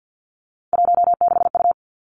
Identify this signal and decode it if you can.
Morse code, heard as a keyed tone, about 50 wpm, 700 Hz. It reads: 16U
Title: Three Challenging Letter-Number Combinations 50wpm